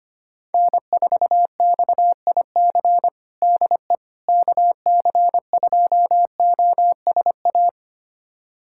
Morse code, heard as a keyed tone, about 25 wpm, 700 Hz. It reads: N4XIC DE KC2OSA
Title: USA Callsign DE Callsign 25wpm